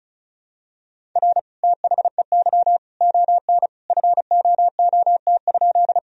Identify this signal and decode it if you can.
Morse code, heard as a keyed tone, about 34 wpm, 700 Hz. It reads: R THEY ON FOOT?
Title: Sentences from Top 500 Words and Abbreviations 35wpm